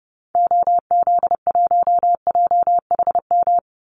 Morse code, heard as a keyed tone, about 30 wpm, 700 Hz. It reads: OZ1JHM
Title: Call Signs Repeated 2x - 30wpm